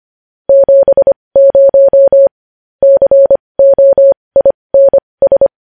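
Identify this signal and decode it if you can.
Morse code, heard as a keyed tone, about 25 wpm, spaced as for 20 wpm, 550 Hz. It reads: 70 COINS
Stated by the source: Morse practice sending news headlines at 20 wpm.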